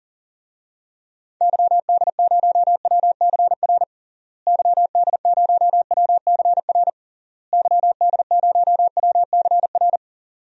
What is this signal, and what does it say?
Morse code, heard as a keyed tone, about 40 wpm, 700 Hz. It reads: YD0WCR YD0WCR YD0WCR
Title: DXCC-100 Call Signs - 6 Characters - Repeated 3x - 40wpm